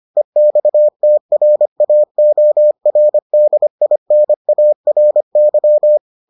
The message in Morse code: EXTRAORDINARY

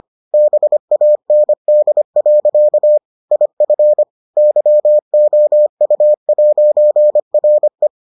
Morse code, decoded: BAND. IF YOU'RE

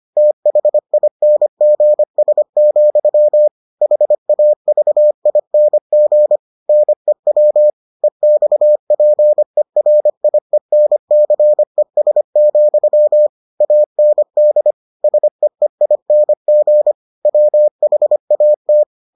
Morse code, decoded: THINGS, HAVING NEW EXPERIENCES, AND SEEING WHAT

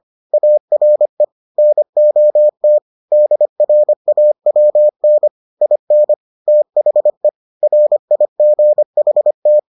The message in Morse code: ARE NOT DRAWN IN THE RIGHT

